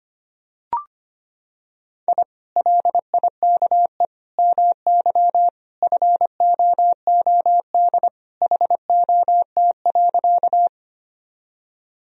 Morse code, heard as a keyed tone, about 25 wpm, 700 Hz. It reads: I LIKE MY FOOD HOT.
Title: Sentences from Top 1000 Words - Review Part 3 - 25wpm